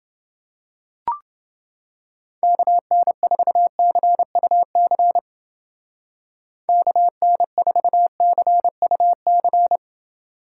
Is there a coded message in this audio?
Yes: Morse code from 2.4 to 9.8 s, heard as a keyed tone, about 29 wpm, 700 Hz. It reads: KN4CUC KN4CUC